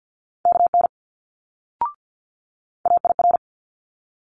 Morse code, heard as a keyed tone, about 50 wpm, 700 Hz. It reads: KN AIR
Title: Multi-letter Phonemes Quick Fire 50wpm